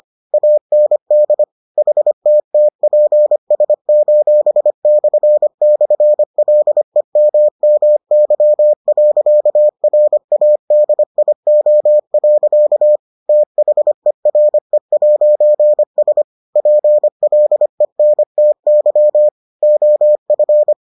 Morse code, heard as a keyed tone, about 25 wpm, 600 Hz. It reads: AND HTTPS://LEMMY.RADIO. THERE'S PLENTY OF